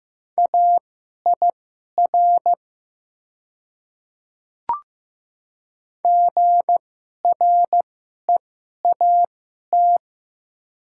Morse code, heard as a keyed tone, about 15 wpm, spaced as for 11 wpm, 700 Hz. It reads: AIR GREAT
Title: 2x Character Spacing - Top 200 Words - 15wpm